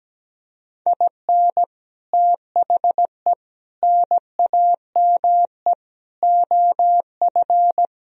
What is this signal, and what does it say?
Morse code, heard as a keyed tone, about 17 wpm, 700 Hz. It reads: IN THE NAME OF
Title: Speed-Racing - Sets of 4 Words 17wpm